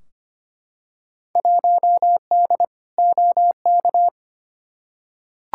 Morse code, heard as a keyed tone, about 25 wpm, 700 Hz. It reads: 1D OK
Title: Field Day Exchanges - 25wpm